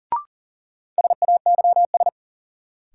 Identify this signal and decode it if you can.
Morse code, heard as a keyed tone, about 40 wpm, 700 Hz. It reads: SAYS
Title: Sets of 2 Words - New Word Drill - 40wpm